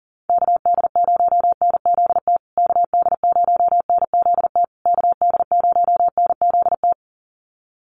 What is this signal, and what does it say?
Morse code, heard as a keyed tone, about 40 wpm, 700 Hz. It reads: KD0NZT KD0NZT KD0NZT
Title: USA Call Signs Repeated 3x 40wpm